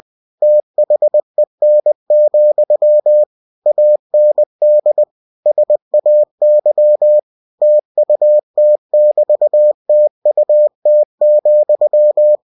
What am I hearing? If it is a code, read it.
Morse code, heard as a keyed tone, about 20 wpm, 600 Hz. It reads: THEN, AND SAY TUT<BT>TUT,